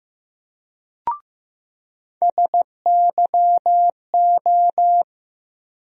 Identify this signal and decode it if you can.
Morse code, heard as a keyed tone, about 15 wpm, 700 Hz. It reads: SYO
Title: USA Call Sign Postfixes - 3 Letters 15wpm